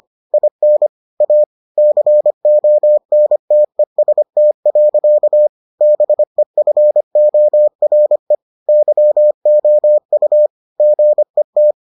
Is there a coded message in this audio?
Yes: Morse code from 0.3 to 11.7 s, heard as a keyed tone, about 25 wpm, 600 Hz. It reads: IN A CONTEST. BEFORE YOU GET